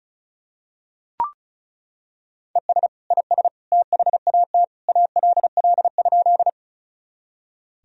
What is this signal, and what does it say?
Morse code, heard as a keyed tone, about 34 wpm, 700 Hz. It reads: ES IS THAT ALL?
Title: Sentences from Top 100 Words and Abbreviations 35wpm